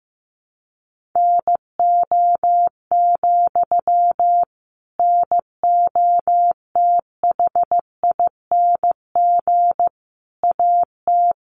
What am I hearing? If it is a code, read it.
Morse code, heard as a keyed tone, about 15 wpm, 700 Hz. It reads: NO, NOTHING AT
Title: Sentences from Top 500 Words 15wpm